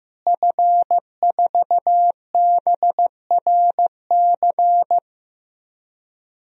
Morse code, heard as a keyed tone, about 15 wpm, 700 Hz. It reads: F4BRC
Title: DXCC-100 Call Signs - 5 Characters - Repeated 2x - 15wpm